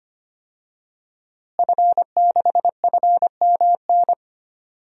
Morse code, heard as a keyed tone, about 25 wpm, 700 Hz. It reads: F6FMN
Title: DXCC-100 Call Signs - 5 Characters - Repeated 3x - 25wpm